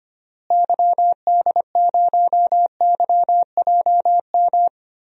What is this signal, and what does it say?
Morse code, heard as a keyed tone, about 25 wpm, 700 Hz. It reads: YD0YJM
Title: DXCC-100 Call Signs - 6 Characters - Repeated 3x - 25wpm